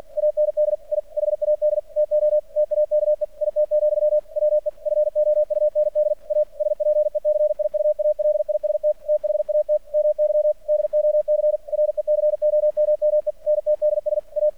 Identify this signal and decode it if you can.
Morse code, heard as a keyed tone, about 48 wpm, 600 Hz. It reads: AND I HAD TO TAKE IT, WE FOUND A SPECIUALIST THAT MY DOC RECOMMENTDS A
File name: archived / CRSnet